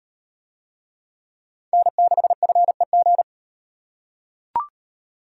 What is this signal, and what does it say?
Morse code, heard as a keyed tone, about 35 wpm, 700 Hz. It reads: N6FEG